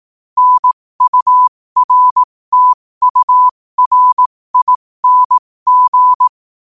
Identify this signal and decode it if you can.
Morse code, heard as a keyed tone, about 18 wpm, spaced as for 15 wpm, 1000 Hz. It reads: NURTURING
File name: qotdfiles / 15wpm